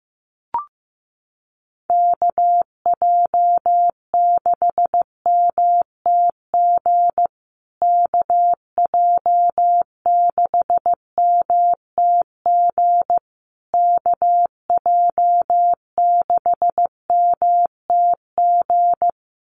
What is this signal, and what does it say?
Morse code, heard as a keyed tone, about 15 wpm, 700 Hz. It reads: KJ6MTG KJ6MTG KJ6MTG
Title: USA Call Signs Repeated 3x 15wpm